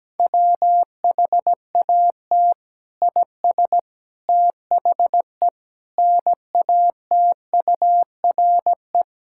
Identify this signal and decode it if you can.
Morse code, heard as a keyed tone, about 17 wpm, 700 Hz. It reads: WHAT IS THE NATURE
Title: Sentences from Top 1000 Words - Review Part 5 - 17wpm